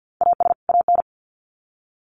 Morse code, heard as a keyed tone, about 50 wpm, 700 Hz. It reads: AS AN
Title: Sets of 2 Words 50wpm